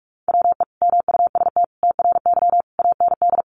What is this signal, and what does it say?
Morse code, heard as a keyed tone, about 44 wpm, 700 Hz. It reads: WE MUST TRY AND
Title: Sentences from Top 1200 Words 45wpm